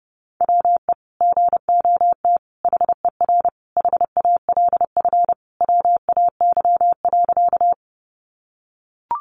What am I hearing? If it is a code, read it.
Morse code, heard as a keyed tone, about 30 wpm, 700 Hz. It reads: WE GOT HER HALF WAY.
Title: Sentences from Top 400 Words 30wpm